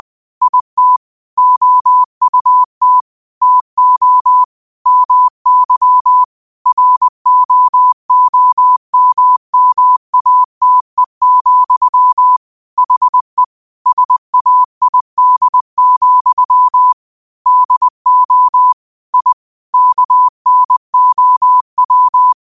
Morse code, heard as a keyed tone, about 20 wpm, 1000 Hz. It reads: IT OUT TO MY ROOMMATE, HE SAID, DO I KNOW
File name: 2024-09-13-20wpm